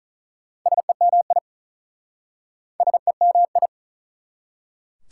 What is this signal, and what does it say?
Morse code, heard as a keyed tone, about 45 wpm, 690 Hz. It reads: SEMI SEMI